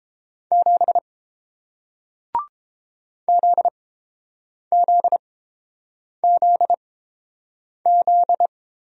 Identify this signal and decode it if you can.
Morse code, heard as a keyed tone, about 30 wpm, spaced as for 7 wpm, 690 Hz. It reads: 7 ZZZZ